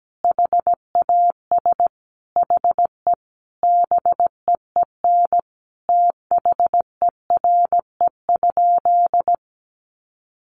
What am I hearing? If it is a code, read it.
Morse code, heard as a keyed tone, about 17 wpm, 700 Hz. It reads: HAS HE BEEN THERE?
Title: Sentences from Top 100 Words 17wpm